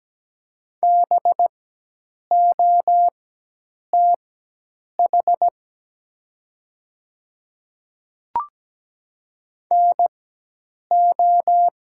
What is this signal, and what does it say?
Morse code, heard as a keyed tone, about 17 wpm, spaced as for 8 wpm, 700 Hz. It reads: BOTH NO
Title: 4x Character Spacing - Top 500 Words - 17wpm